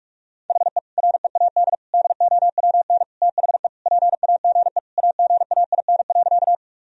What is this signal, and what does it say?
Morse code, heard as a keyed tone, about 46 wpm, 700 Hz. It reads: HE READ DOWN THE PAGE AGAIN.
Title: Sentences from Top 1000 Words - Review Part 1 - 45wpm